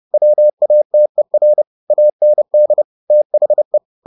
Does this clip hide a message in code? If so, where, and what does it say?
Yes: Morse code from 0.1 to 3.8 s, heard as a keyed tone, about 30 wpm, 600 Hz. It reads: WATER AND THE